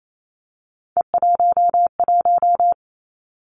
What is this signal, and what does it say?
Morse code, heard as a keyed tone, about 28 wpm, 700 Hz. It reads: E11
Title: Three Challenging Letter-Number Combinations 28wpm